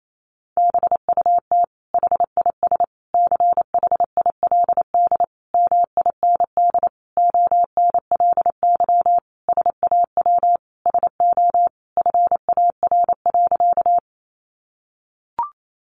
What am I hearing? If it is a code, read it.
Morse code, heard as a keyed tone, about 28 wpm, 700 Hz. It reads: BUT HIS CHILD MIND ONLY SAW SO FAR.
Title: Sentences from Top 800 Words 28wpm